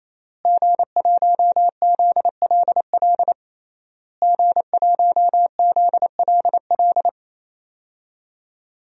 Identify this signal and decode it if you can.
Morse code, heard as a keyed tone, about 27 wpm, 700 Hz. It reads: G1ZLL G1ZLL
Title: DXCC-100 Call Signs - 5 Characters - Repeated 3x - 28wpm